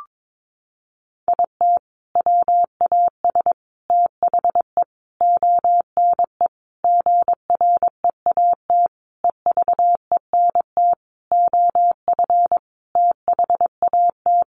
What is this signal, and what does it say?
Morse code, heard as a keyed tone, about 22 wpm, 700 Hz. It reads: IT WAS THE ONE GREAT EVENT OF THAT
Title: Sentences from Top 1400 Words 22wpm